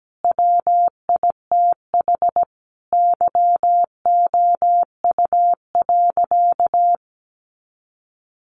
Morse code, heard as a keyed tone, about 17 wpm, 700 Hz. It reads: WITH YOU.
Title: Sentences from Top 500 Words 17wpm